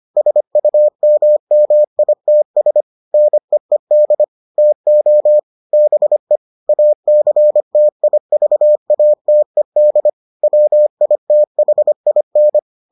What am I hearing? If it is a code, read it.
Morse code, heard as a keyed tone, about 25 wpm, 600 Hz. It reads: SUMMITS NEED TO BE ACTIVATED WITHIN